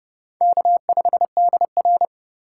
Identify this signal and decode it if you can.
Morse code, heard as a keyed tone, about 32 wpm, 700 Hz. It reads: K5DR
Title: USA Call Signs - 4 Characters - 2x Character Spacing - 30wpm